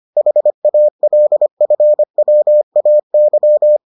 Morse code, heard as a keyed tone, about 25 wpm, 600 Hz. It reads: HALFWAY